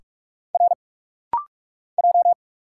Morse code, heard as a keyed tone, about 46 wpm, 690 Hz. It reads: R J
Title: Single Letters - Rapid-Fire 45wpm